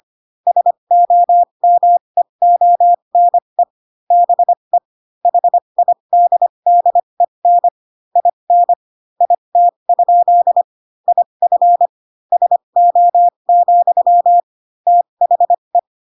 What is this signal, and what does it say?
Morse code, heard as a keyed tone, about 25 wpm, spaced as for 22 wpm, 700 Hz. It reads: SOMEONE BE HIDDEN IN IT? IF SO, THE